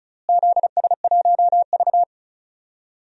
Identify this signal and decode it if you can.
Morse code, heard as a keyed tone, about 34 wpm, 700 Hz. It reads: ZS1V
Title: DXCC-100 Call Signs - 4 Characters - Repeated 3x - 35wpm